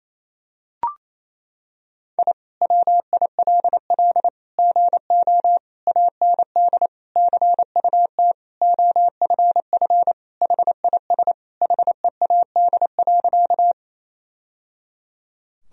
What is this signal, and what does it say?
Morse code, heard as a keyed tone, about 28 wpm, 700 Hz. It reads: I WILL GO AND CUT OFF HIS HEAD.
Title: Sentences from Top 1000 Words - Review Part 1 - 28wpm